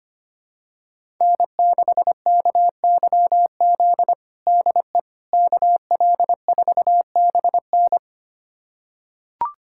Morse code, heard as a keyed tone, about 25 wpm, 700 Hz. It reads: N6KYZ DE KL4BN
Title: USA Callsign DE Callsign - 5 Characters - 25wpm